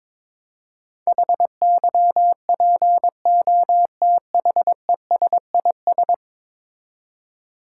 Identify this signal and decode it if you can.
Morse code, heard as a keyed tone, about 22 wpm, 700 Hz. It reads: HYPOTHESIS
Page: morse-code-ninja-practice-22wpm